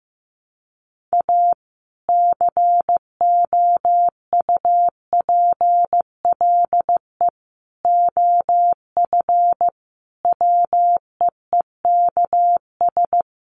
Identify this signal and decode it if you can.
Morse code, heard as a keyed tone, about 15 wpm, 700 Hz. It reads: A COUPLE OF WEEKS
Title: Sets of 5 Words 15wpm